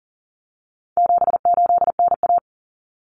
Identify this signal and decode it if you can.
Morse code, heard as a keyed tone, about 40 wpm, 700 Hz. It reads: 78NA
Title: Common Cut Numbers - 4-Digit 40wpm